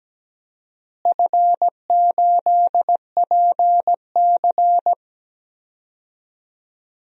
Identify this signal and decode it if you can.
Morse code, heard as a keyed tone, about 17 wpm, 700 Hz. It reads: F8PC
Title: DXCC-100 Call Signs - 4 Characters - Repeated 3x - 17wpm